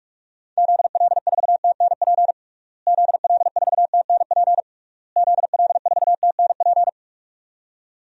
Morse code, heard as a keyed tone, about 46 wpm, 700 Hz. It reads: ZL4TNP ZL4TNP ZL4TNP
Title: DXCC-100 Call Signs - 6 Characters - Repeated 3x - 45wpm